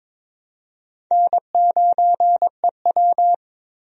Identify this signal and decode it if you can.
Morse code, heard as a keyed tone, about 22 wpm, 700 Hz. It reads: N9EW
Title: USA Call Signs - 4 Characters - 22wpm